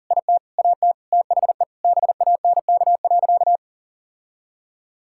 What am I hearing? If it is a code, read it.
Morse code, heard as a keyed tone, about 40 wpm, 700 Hz. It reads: IT AT THE BANK.
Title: Sentences from Top 1000 Words - Review Part 5 - 40wpm